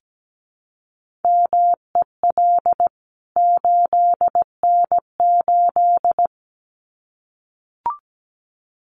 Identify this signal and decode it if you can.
Morse code, heard as a keyed tone, about 17 wpm, 700 Hz. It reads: MEL 8N8
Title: CWT Exchange 17wpm